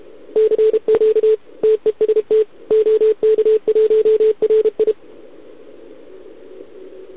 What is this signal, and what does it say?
Morse code, heard as a keyed tone, about 32 wpm, 440 Hz. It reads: CQ TEST OK1RI
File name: ARCHIV / OKDX2020 / OK1RIb